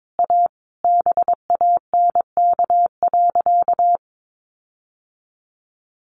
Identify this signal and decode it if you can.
Morse code, heard as a keyed tone, about 22 wpm, 700 Hz. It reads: A BANK.
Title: Sentences from Top 1500 Words 22wpm